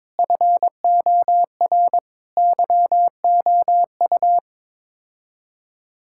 Morse code, heard as a keyed tone, about 22 wpm, 700 Hz. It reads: FOR YOU
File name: Sets of 9 Words - 22wpm